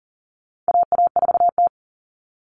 Morse code, heard as a keyed tone, about 40 wpm, 700 Hz. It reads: AA4T
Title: All Cut Numbers - 4-Digit 40wpm